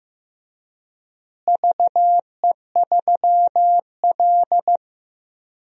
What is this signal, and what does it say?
Morse code, heard as a keyed tone, about 15 wpm, 700 Hz. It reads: VE3L
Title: DXCC-100 Call Signs - 4 Characters - Repeated 3x - 15wpm